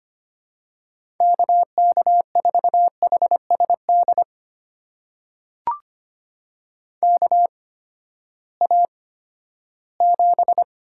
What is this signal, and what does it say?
Morse code, heard as a keyed tone, about 25 wpm, spaced as for 14 wpm, 700 Hz. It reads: KK4HSD KA7
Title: USA Call Signs - 6 Characters - 8x Character Spacing - 25wpm